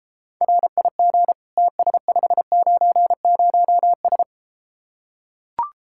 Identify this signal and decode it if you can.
Morse code, heard as a keyed tone, about 34 wpm, 700 Hz. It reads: RIG TS590S